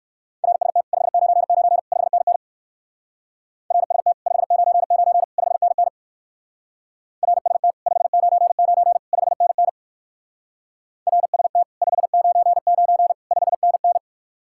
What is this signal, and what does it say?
Morse code, heard as a keyed tone, about 57 wpm, 700 Hz. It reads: RST 599 5NN RST 599 5NN RST 599 5NN RST 599 5NN